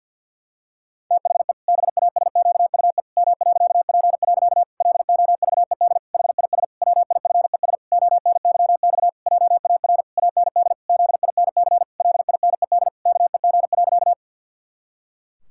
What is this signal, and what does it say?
Morse code, heard as a keyed tone, about 50 wpm, 700 Hz. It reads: THE BASQUE M.P. LOVED HIS WIFES ONYX JAR AND ZINC LINED KEG.
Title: Pangram Challenge v2 - 50wpm